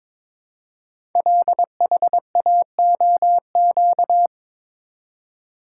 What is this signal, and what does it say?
Morse code, heard as a keyed tone, about 22 wpm, 700 Hz. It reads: LHAOQ
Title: Code Groups - 5 Letters - 22wpm